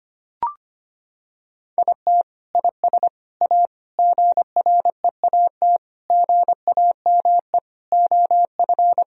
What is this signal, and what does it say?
Morse code, heard as a keyed tone, about 25 wpm, 700 Hz. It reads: IT IS A GREAT GAME OF
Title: Sentences from Top 1000 Words - Review Part 3 - 25wpm